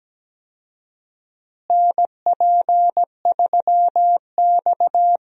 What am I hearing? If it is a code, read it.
Morse code, heard as a keyed tone, about 17 wpm, 700 Hz. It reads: NP3X
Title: USA Call Signs - 4 Characters - Repeated 2x - 17wpm